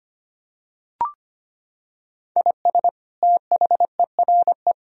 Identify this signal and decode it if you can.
Morse code, heard as a keyed tone, about 25 wpm, 700 Hz. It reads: IS THERE
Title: Sentences from Top 1000 Words - Review Part 1 - 25wpm